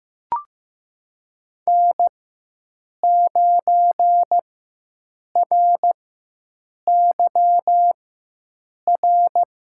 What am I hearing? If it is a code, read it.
Morse code, heard as a keyed tone, about 15 wpm, spaced as for 7 wpm, 700 Hz. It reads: N9RYR
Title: USA Call Signs - 4x Character Spacing - 15wpm